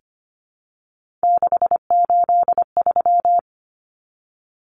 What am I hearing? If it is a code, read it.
Morse code, heard as a keyed tone, about 25 wpm, 690 Hz. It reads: 683
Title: Numbers - 3-Digit 25wpm